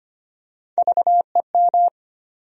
Morse code, heard as a keyed tone, about 25 wpm, 700 Hz. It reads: VEM
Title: USA Call Sign Postfixes - 3 Letters 25wpm